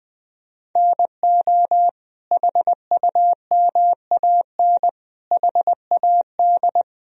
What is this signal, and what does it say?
Morse code, heard as a keyed tone, about 19 wpm, 700 Hz. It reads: NO HUMAN HAD
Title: Sentences from Top 1000 Words - Review Part 3 - 20wpm